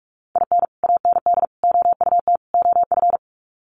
Morse code, heard as a keyed tone, about 44 wpm, 700 Hz. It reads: IN AND OUT OF
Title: Sets of 4 Words 45wpm